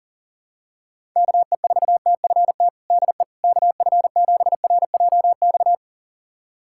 Morse code, heard as a keyed tone, about 40 wpm, 700 Hz. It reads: KE4TFT DE KF7RJX
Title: USA Callsign DE Callsign - 6 Characters - 40wpm